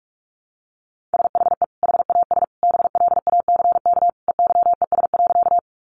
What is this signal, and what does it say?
Morse code, heard as a keyed tone, about 44 wpm, 700 Hz. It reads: SHE HAS BLACK EYES.
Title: Easy Phrases 45wpm